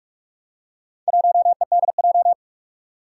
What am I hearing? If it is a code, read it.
Morse code, heard as a keyed tone, about 44 wpm, 700 Hz. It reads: 1EDJ
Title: Four Challenging Letter-Number Combinations 45wpm